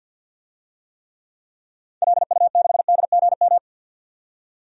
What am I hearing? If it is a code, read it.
Morse code, heard as a keyed tone, about 53 wpm, 700 Hz. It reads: LU6DGM
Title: DXCC-100 Call Signs - 6 Characters - 50wpm